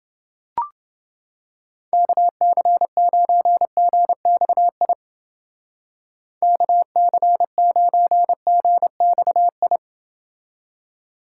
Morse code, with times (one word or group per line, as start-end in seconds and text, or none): 1.9-4.9 KC9GXI
6.4-9.8 KC9GXI